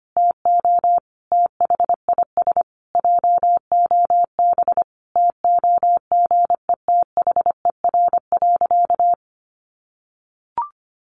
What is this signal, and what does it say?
Morse code, heard as a keyed tone, about 25 wpm, 700 Hz. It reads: TO THIS JOB TOGETHER.
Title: Sentences from Top 1300 Words 25wpm